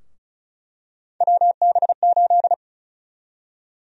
Morse code, heard as a keyed tone, about 34 wpm, 700 Hz. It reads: WB8
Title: USA Call Sign Prefixes - 35wpm